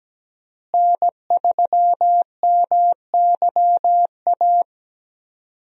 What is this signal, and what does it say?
Morse code, heard as a keyed tone, about 17 wpm, 700 Hz. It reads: N3MYA
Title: USA Call Signs - 5 Characters - Repeated 2x - 17wpm